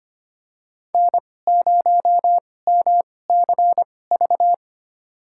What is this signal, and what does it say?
Morse code, heard as a keyed tone, about 25 wpm, spaced as for 18 wpm, 700 Hz. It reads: N0MCV
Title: USA Call Signs - 2x Character Spacing - 25wpm